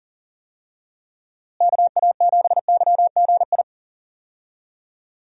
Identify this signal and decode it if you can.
Morse code, heard as a keyed tone, about 40 wpm, 700 Hz. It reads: KA7YGI
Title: USA Call Signs - 6 Characters - Repeated 3x - 40wpm